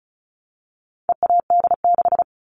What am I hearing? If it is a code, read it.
Morse code, heard as a keyed tone, about 34 wpm, 700 Hz. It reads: EAD6
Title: All Cut Numbers - 4-Digit 35wpm